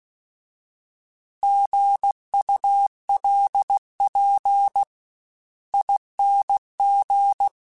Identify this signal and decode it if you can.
Morse code, heard as a keyed tone, about 16 wpm, spaced as for 13 wpm, 800 Hz. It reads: GULPING
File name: Alphabetical / Alphabetical mp3s / Gs @15